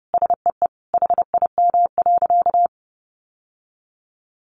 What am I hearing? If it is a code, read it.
Morse code, heard as a keyed tone, about 30 wpm, 700 Hz. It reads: SEE HIM.
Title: Sentences from Top 1500 Words 30wpm